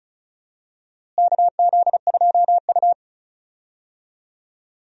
Morse code, heard as a keyed tone, about 36 wpm, 700 Hz. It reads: KZ2U
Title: USA Call Signs - 4 Characters - Repeated 2x - 35wpm